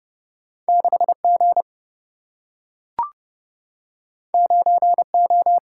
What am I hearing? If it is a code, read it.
Morse code, heard as a keyed tone, about 30 wpm, 700 Hz. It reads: 6G 9O
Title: Single-Letter Single-Number 30wpm